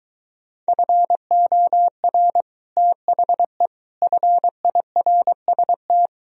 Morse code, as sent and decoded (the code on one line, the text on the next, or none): ..-. --- .-. / - .... / ..-. .. .-. ... -
FOR THE FIRST